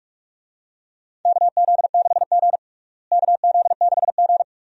Morse code, heard as a keyed tone, about 46 wpm, 700 Hz. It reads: KZ6G KZ6G
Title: USA Call Signs - 4 Characters - Repeated 2x - 45wpm